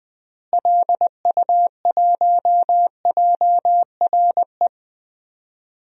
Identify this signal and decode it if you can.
Morse code, heard as a keyed tone, about 20 wpm, 700 Hz. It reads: LU1JRE
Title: DXCC-100 Call Signs - 6 Characters - Repeated 3x - 20wpm